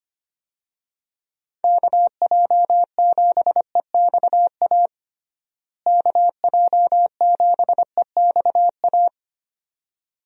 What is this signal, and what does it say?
Morse code, heard as a keyed tone, about 25 wpm, 700 Hz. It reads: KJ7EXA KJ7EXA
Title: USA Call Signs - 6 Characters - Repeated 2x - 25wpm